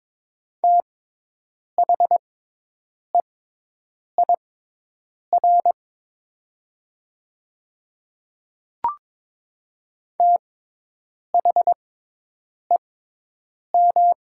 Morse code, heard as a keyed tone, about 22 wpm, spaced as for 8 wpm, 700 Hz. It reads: THEIR THEM
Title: 6x Character Spacing - Top 100 Words - 22wpm